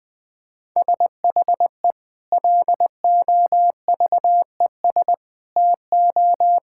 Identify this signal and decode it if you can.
Morse code, heard as a keyed tone, about 20 wpm, 700 Hz. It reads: SHE LOVES TO
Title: Sentences with 2-Word Phrases - 20wpm